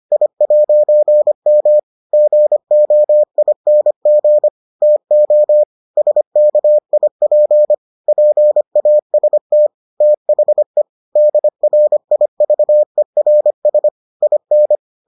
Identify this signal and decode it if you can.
Morse code, heard as a keyed tone, about 25 wpm, 600 Hz. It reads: I'M GOING TO SKIP PAST THE DRIVERS IN